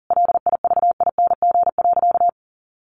Morse code, heard as a keyed tone, about 40 wpm, 700 Hz. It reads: LIVING.
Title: Sentences from Top 1500 Words 40wpm